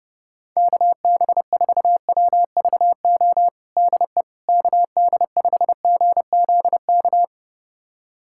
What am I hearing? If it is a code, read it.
Morse code, heard as a keyed tone, about 30 wpm, 700 Hz. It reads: KB4WVO DE KD5GZK
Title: USA Callsign DE Callsign 30wpm